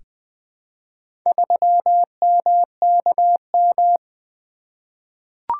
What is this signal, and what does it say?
Morse code, heard as a keyed tone, about 20 wpm, 700 Hz. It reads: 3MKM
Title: Four Letter-Number Combinations - 20wpm